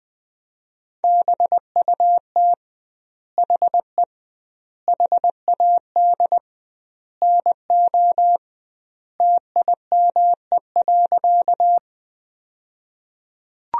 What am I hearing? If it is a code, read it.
Morse code, heard as a keyed tone, about 21 wpm, 700 Hz. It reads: BUT HE HAD NO TIME.
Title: 2x Word-Spacing - Sentences from Top 100 Words - 20wpm